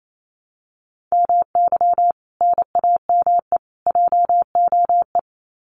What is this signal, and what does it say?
Morse code, heard as a keyed tone, about 28 wpm, 700 Hz. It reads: MY NAME JOE
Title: Common QSO Phrases 28wpm